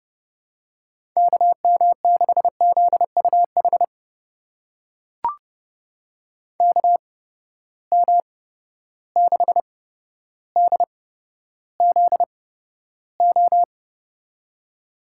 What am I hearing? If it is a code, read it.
Morse code, heard as a keyed tone, about 30 wpm, spaced as for 13 wpm, 700 Hz. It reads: KM6ZUH KM6DZO
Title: USA Call Signs - 6 Characters - 8x Character Spacing - 30wpm